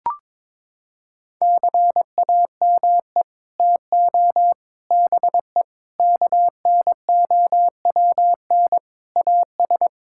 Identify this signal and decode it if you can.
Morse code, heard as a keyed tone, about 22 wpm, 700 Hz. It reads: CAME TO BE KNOWN AS
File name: practice / 22wpm / Sets of 6 Words - 22wpm